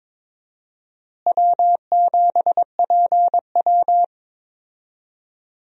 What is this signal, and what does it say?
Morse code, heard as a keyed tone, about 23 wpm, 700 Hz. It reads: W7PW
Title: USA Call Signs - 4 Characters - Repeated 3x - 22wpm